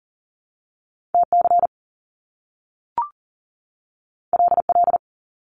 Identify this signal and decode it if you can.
Morse code, heard as a keyed tone, about 38 wpm, 700 Hz. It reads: TC LL
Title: Top Two Letter Combinations in Words 40wpm